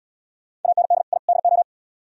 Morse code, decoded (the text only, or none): AND I DO